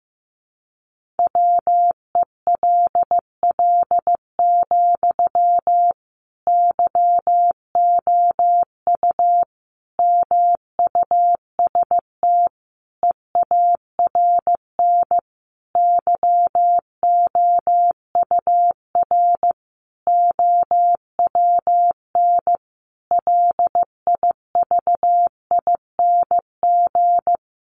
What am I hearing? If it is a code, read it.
Morse code, heard as a keyed tone, about 15 wpm, 700 Hz. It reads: WELL, YOU MUST EARN YOUR OWN LIVING
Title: Sentences from Top 1500 Words 15wpm